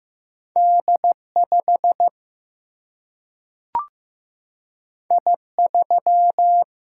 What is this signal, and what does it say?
Morse code, heard as a keyed tone, about 15 wpm, 700 Hz. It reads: D5 I3
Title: Single-Letter Single-Number 15wpm